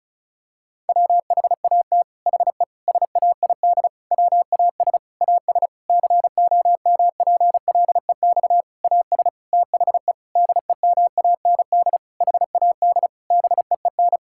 Morse code, heard as a keyed tone, about 34 wpm, 700 Hz. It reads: WHAT HE SAID WAS AS COMPLEX AS THE DEMAND HAD BEEN
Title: Sentences from Top 1000 Words - Review Part 6 - 35wpm